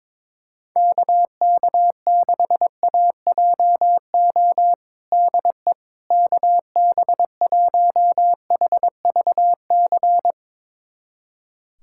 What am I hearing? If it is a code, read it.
Morse code, heard as a keyed tone, about 22 wpm, 700 Hz. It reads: KK6AJO DE KB1HVC
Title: USA Callsign DE Callsign 22wpm